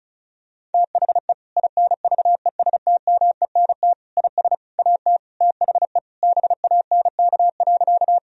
Morse code, heard as a keyed tone, about 34 wpm, 700 Hz. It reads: THE INVESTMENT IS AT THE BANK.
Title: Sentences from Top 1000 Words - Review Part 4 - 35wpm